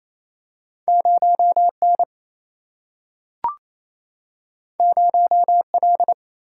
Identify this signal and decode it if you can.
Morse code, heard as a keyed tone, about 28 wpm, 700 Hz. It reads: 0 N 0 L
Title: Single-Letter Single-Number 28wpm